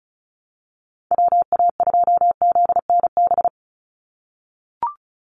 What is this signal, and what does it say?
Morse code, heard as a keyed tone, about 34 wpm, 700 Hz. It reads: WA2ZNB
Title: USA Call Signs 35wpm